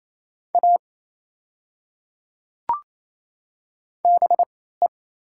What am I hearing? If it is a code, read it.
Morse code, heard as a keyed tone, about 28 wpm, spaced as for 16 wpm, 700 Hz. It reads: A BE
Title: 3x Character Spacing - Top 100 Words - 28wpm